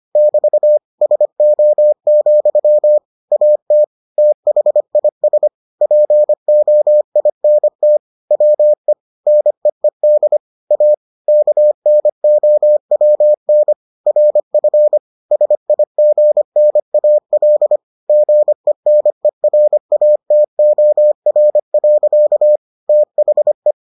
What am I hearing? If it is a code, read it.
Morse code, heard as a keyed tone, about 25 wpm, 600 Hz. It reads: <BT> SO, AT THIS POINT WE NEED A KNOWN RF SIGNAL GENERATOR. THE